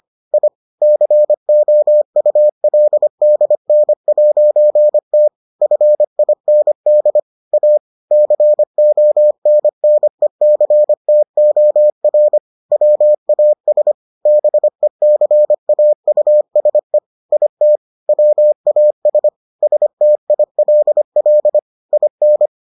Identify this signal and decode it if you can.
Morse code, heard as a keyed tone, about 25 wpm, 600 Hz. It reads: I COULDN'T FIND A CONNECTOR WAS BECAUSE IT WAS STILL IN